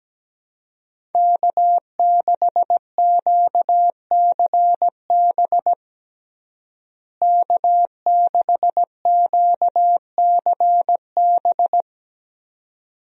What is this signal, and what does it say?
Morse code, heard as a keyed tone, about 17 wpm, 700 Hz. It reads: K6QCB K6QCB
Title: USA Call Signs - 5 Characters - Repeated 2x - 17wpm